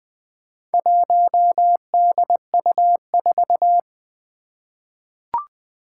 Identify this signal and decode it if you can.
Morse code, heard as a keyed tone, about 20 wpm, 700 Hz. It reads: 1DU4
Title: Four Challenging Letter-Number Combinations 20wpm